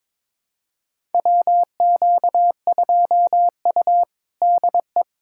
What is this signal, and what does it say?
Morse code, heard as a keyed tone, about 22 wpm, 700 Hz. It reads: WQ2U DE
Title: USA Callsign DE Callsign - 4 Characters - 22wpm